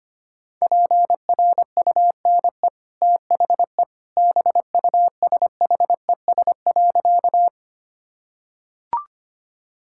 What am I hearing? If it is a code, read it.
Morse code, heard as a keyed tone, about 25 wpm, 700 Hz. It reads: PRUNE THE BUSHES.
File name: Easy Sentences with Hints - 25wpm